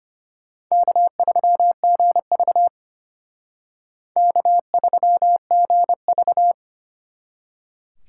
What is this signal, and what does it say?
Morse code, heard as a keyed tone, about 32 wpm, 700 Hz. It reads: K3GV K3GV